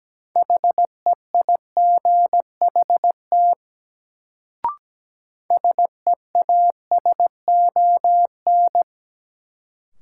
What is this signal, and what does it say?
Morse code, heard as a keyed tone, about 17 wpm, 700 Hz. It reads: HEIGHT SEASON
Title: Sets of 5 Words - New Word Drill - 17wpm